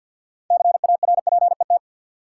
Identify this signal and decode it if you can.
Morse code, heard as a keyed tone, about 47 wpm, 700 Hz. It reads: CARPET
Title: Six Letter Words - 50wpm